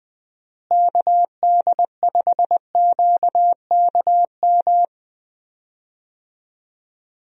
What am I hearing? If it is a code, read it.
Morse code, heard as a keyed tone, about 20 wpm, 700 Hz. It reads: KD5QKM
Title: USA Call Signs - 6 Characters - Repeated 3x - 20wpm